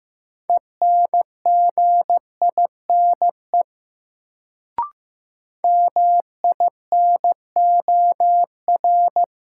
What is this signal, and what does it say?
Morse code, heard as a keyed tone, about 16 wpm, 700 Hz. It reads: ENGINE MINOR
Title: Sentences from Top 1500 Words - New Word Drill Alt - 15wpm